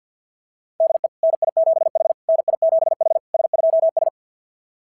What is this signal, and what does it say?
Morse code, heard as a keyed tone, about 47 wpm, 690 Hz. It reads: DE NI7H NI7H S2S
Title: SOTA QSO Practice 50wpm